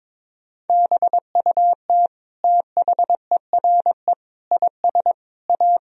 Morse code, heard as a keyed tone, about 22 wpm, 700 Hz. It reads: BUT THERE IS A
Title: Sentences from Top 1000 Words - Review Part 2 - 22wpm